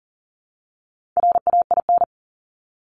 Morse code, heard as a keyed tone, about 40 wpm, 700 Hz. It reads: RAIN
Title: Top 500 Words Repeated 40wpm